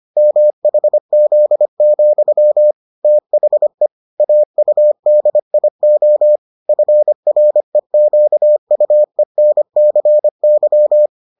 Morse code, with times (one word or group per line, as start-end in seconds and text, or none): 0.2-2.7 MHZ,
3.0-3.9 THE
4.2-6.4 AUDIO
6.7-11.1 FREQUENCY